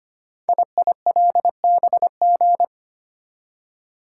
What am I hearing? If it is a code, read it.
Morse code, heard as a keyed tone, about 25 wpm, 700 Hz. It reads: IILBG
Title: Code Groups - 5 Letters - 25wpm